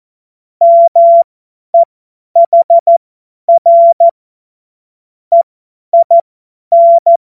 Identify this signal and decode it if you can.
Morse code, heard as a keyed tone, about 14 wpm, spaced as for 10 wpm, 680 Hz. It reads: MEHR EIN
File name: Morsekurs / Witze / Witz 07.November 2023 50 F70